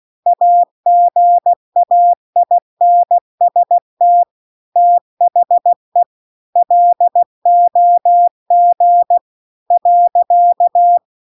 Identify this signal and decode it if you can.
Morse code, heard as a keyed tone, about 16 wpm, 700 Hz. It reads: AGAINST THE LOG .
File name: CW-The-Big-Twoheaded-River-2-16WPM